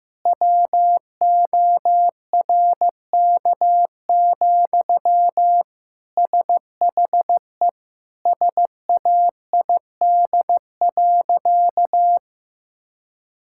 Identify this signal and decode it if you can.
Morse code, heard as a keyed tone, about 16 wpm, 700 Hz. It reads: WORK, SHE SAID.
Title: Sentences from Top 1000 Words - Review Part 6 - 15wpm